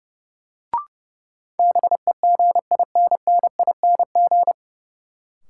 Morse code, heard as a keyed tone, about 30 wpm, 700 Hz. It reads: BEGINNING
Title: Sentences from Top 800 Words - New Word Drill - 30wpm